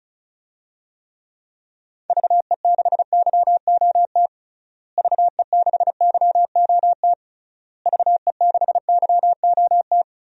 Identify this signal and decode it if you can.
Morse code, heard as a keyed tone, about 36 wpm, 700 Hz. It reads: VE6YOT VE6YOT VE6YOT
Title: DXCC-100 Call Signs - 6 Characters - Repeated 3x - 35wpm